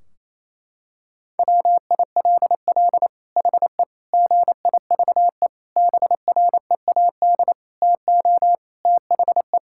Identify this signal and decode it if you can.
Morse code, heard as a keyed tone, about 28 wpm, 700 Hz. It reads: WILL HE GIVE BREAD TO THE
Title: Sentences from Top 1600 Words 28wpm